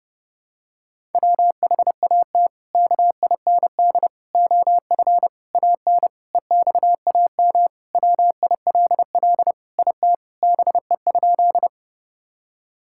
Morse code, heard as a keyed tone, about 32 wpm, 700 Hz. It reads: WHAT KIND OF AN EXAM WILL IT BE?
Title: Sentences from Top 1000 Words - Review Part 5 - 30wpm